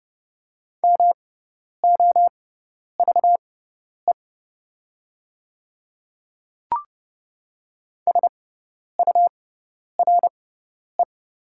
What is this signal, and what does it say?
Morse code, heard as a keyed tone, about 29 wpm, spaced as for 10 wpm, 690 Hz. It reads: MOVE SURE
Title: 6x Character Spacing - Top 500 Words - 30wpm